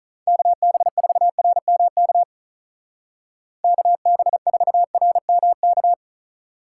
Morse code, heard as a keyed tone, about 43 wpm, 700 Hz. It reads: KB4RMK KB4RMK
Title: Speed Racing - USA 2x3 Call Signs 35wpm